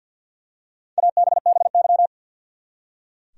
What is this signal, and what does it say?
Morse code, heard as a keyed tone, about 46 wpm, 700 Hz. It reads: ABBY
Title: USA Common Names - Max 5 Letters - 50wpm